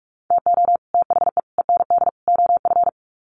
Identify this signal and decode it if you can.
Morse code, heard as a keyed tone, about 44 wpm, 700 Hz. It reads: TO THE END OF
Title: Sets of 4 Words 45wpm